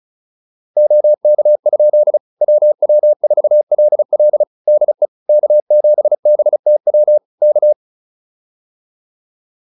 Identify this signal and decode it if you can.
Morse code, heard as a keyed tone, about 34 wpm, 600 Hz. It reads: OK? WW4LL DE K7BTW K
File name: QSO203 35